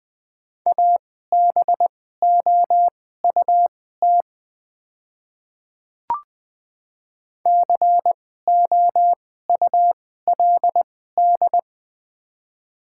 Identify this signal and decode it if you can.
Morse code, heard as a keyed tone, about 20 wpm, spaced as for 14 wpm, 700 Hz. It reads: ABOUT COULD
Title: 2x Character Spacing - Top 100 Words - 20wpm